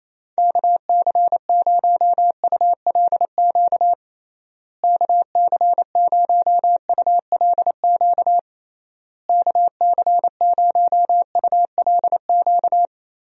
Morse code, heard as a keyed tone, about 28 wpm, 700 Hz. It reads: KC0ULQ KC0ULQ KC0ULQ
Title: USA Call Signs - 6 Characters - Repeated 3x - 28wpm